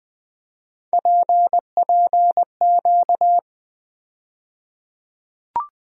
Morse code, heard as a keyed tone, about 20 wpm, 700 Hz. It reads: PPQ
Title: USA Call Sign Postfixes - 3 Letters 20wpm